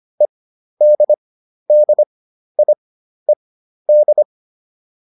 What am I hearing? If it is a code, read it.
Morse code, heard as a keyed tone, about 25 wpm, spaced as for 12 wpm, 600 Hz. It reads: EDDIED
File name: ED6 12